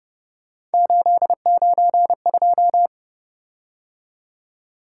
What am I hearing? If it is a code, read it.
Morse code, heard as a keyed tone, about 30 wpm, 700 Hz. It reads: 892
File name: Speed-Racing - Three Letter-Number Combinations - 25wpm